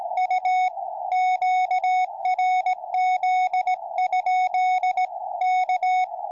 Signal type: Morse code